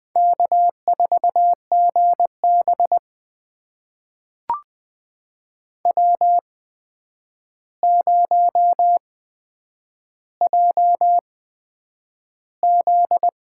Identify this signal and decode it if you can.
Morse code, heard as a keyed tone, about 20 wpm, spaced as for 9 wpm, 700 Hz. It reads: K4GB W0JZ
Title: USA Call Signs - 4 Characters - 8x Character Spacing - 20wpm